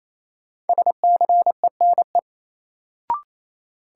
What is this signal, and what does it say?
Morse code, heard as a keyed tone, about 28 wpm, 700 Hz. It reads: SCENE
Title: Sentences from Top 1100 Words - New Word Drill - 28wpm